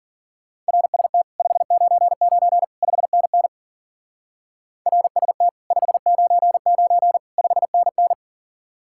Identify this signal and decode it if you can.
Morse code, heard as a keyed tone, about 48 wpm, 700 Hz. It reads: RST 599 5NN RST 599 5NN
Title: Speed-Racing Common QSO Phrases - 40wpm